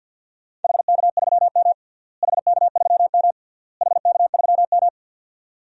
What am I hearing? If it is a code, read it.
Morse code, heard as a keyed tone, about 46 wpm, 700 Hz. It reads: HK3M HK3M HK3M
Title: DXCC-100 Call Signs - 4 Characters - Repeated 3x - 50wpm